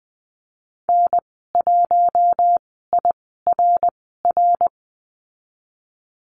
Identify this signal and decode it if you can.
Morse code, heard as a keyed tone, about 20 wpm, 700 Hz. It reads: N 1 I R R
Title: 055 Lesson 10g - Letter r - Call Signs 20wpm